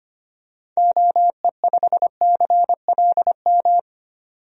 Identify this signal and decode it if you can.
Morse code, heard as a keyed tone, about 25 wpm, 690 Hz. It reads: OE5CLM
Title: DXCC-100 Call Signs - 6 Characters - 25wpm